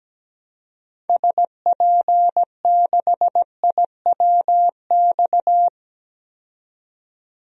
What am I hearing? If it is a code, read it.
Morse code, heard as a keyed tone, about 17 wpm, 700 Hz. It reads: SP6IWX
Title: DXCC-100 Call Signs - 6 Characters - 17wpm